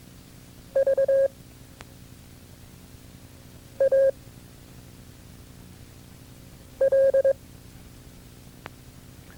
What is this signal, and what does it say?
Morse code, heard as a keyed tone, about 22 wpm, 540 Hz. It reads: V A L
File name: FreeFiles / CW-practice / KB6MT-tape2-side1